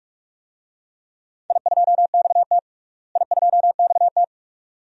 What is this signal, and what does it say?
Morse code, heard as a keyed tone, about 46 wpm, 700 Hz. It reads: I2XT I2XT
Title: DXCC-100 Call Signs - 4 Characters - Repeated 2x - 45wpm